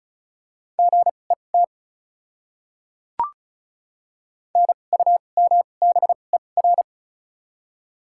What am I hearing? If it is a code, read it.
Morse code, heard as a keyed tone, about 34 wpm, spaced as for 25 wpm, 700 Hz. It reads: GET NUMBER
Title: 2x Character Spacing - Top 300 Words - 35wpm